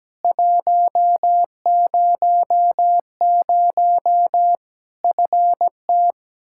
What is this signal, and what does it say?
Morse code, heard as a keyed tone, about 17 wpm, 700 Hz. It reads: 100 FT
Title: Practice QSO - Exchange 2 - Info - 17wpm